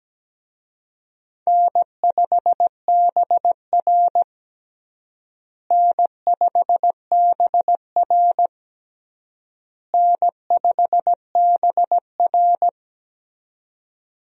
Morse code, heard as a keyed tone, about 17 wpm, 700 Hz. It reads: N5BR N5BR N5BR
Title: USA Call Signs - 4 Characters - Repeated 3x - 17wpm